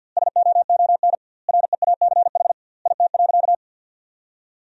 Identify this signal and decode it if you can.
Morse code, heard as a keyed tone, about 50 wpm, 700 Hz. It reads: SOON REACH IT.
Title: 2x Word-Spacing - Sentences from Top 400 Words - 50wpm